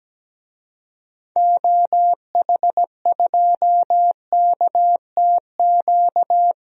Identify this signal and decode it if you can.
Morse code, heard as a keyed tone, about 17 wpm, 700 Hz. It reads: OH2KTQ
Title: DXCC-100 Call Signs - 6 Characters - 17wpm